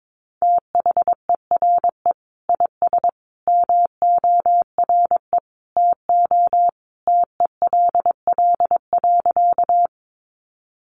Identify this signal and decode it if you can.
Morse code, heard as a keyed tone, about 22 wpm, 700 Hz. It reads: THERE IS MORE TO TELL.
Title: Sentences from Top 200 Words 22wpm